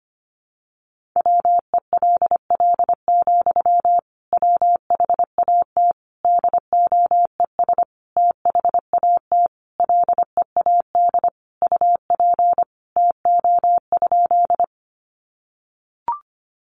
Morse code, heard as a keyed tone, about 25 wpm, 700 Hz. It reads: WELL, WHAT DOES THAT LEAD UP TO?
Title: Sentences from Top 500 Words 25wpm